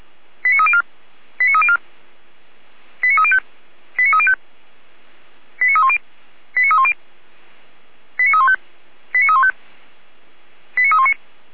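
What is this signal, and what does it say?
Five-tone selective call (ZVEI), heard as quick runs of five tones, tones of about 70 ms. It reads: 89304 89304 89305 89305 89310 89310 89315 89315 89319
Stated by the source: ZVEI five-tone call received on SDR by feuerwehr-orlamuende fire service.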